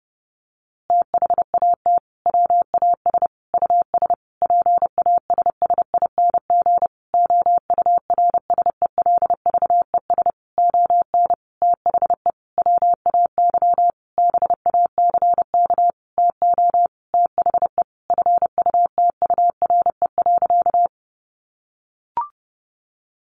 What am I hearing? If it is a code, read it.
Morse code, heard as a keyed tone, about 30 wpm, 700 Hz. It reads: THAT WAS US PASSING OURSELVES ON THE WAY BACK TO THE FUTURE.
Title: A Sound Of Thunder 30wpm